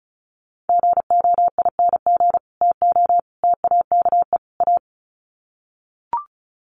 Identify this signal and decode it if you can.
Morse code, heard as a keyed tone, about 34 wpm, 700 Hz. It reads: GOING TO TAKE A
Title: Sets of 4 Words 35wpm